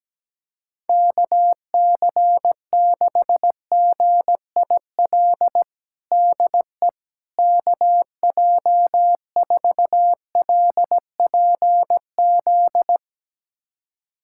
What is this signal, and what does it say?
Morse code, heard as a keyed tone, about 17 wpm, 700 Hz. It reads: KC6GIL DE KJ4LPZ
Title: USA Callsign DE Callsign - 6 Characters - 17wpm